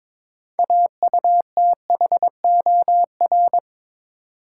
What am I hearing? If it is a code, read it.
Morse code, heard as a keyed tone, about 22 wpm, 700 Hz. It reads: AUTHOR
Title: Six Letter Words - 22wpm